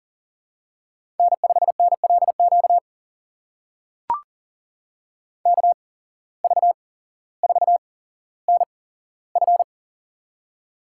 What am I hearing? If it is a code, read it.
Morse code, heard as a keyed tone, about 40 wpm, spaced as for 17 wpm, 700 Hz. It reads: N5NLQ KV4NF
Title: USA Call Signs - 5 Characters - 8x Character Spacing - 40wpm